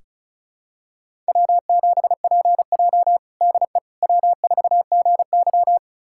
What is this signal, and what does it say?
Morse code, heard as a keyed tone, about 34 wpm, 700 Hz. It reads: W7PJ DE W4GY
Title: USA Callsign DE Callsign - 4 Characters - 35wpm